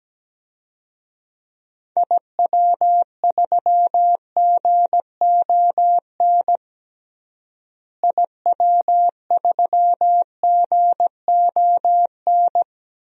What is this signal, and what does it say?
Morse code, heard as a keyed tone, about 17 wpm, 700 Hz. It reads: IW3GON IW3GON
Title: DXCC-100 Call Signs - 6 Characters - Repeated 3x - 17wpm